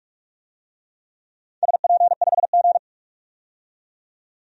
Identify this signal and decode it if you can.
Morse code, heard as a keyed tone, about 46 wpm, 700 Hz. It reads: SP5G
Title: DXCC-100 Call Signs - 4 Characters - 45wpm